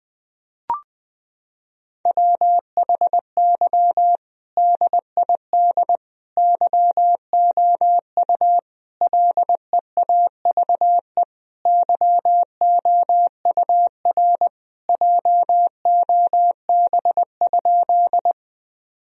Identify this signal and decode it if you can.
Morse code, heard as a keyed tone, about 20 wpm, 700 Hz. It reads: WHY DID YOU LEAVE YOUR JOB?
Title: Sentences from Top 1000 Words - Review Part 3 - 20wpm